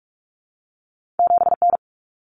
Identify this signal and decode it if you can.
Morse code, heard as a keyed tone, about 46 wpm, 700 Hz. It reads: 7N
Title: Common Cut Numbers - 2-Digit 45wpm